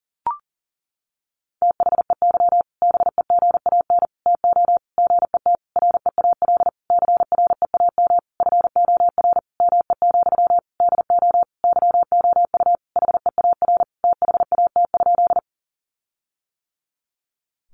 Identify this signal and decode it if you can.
Morse code, heard as a keyed tone, about 40 wpm, 700 Hz. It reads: THEY BEGAN TO GET REAL CREAM FOR ME, DO YOU HEAR THAT?
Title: Sentences from Top 1500 Words 40wpm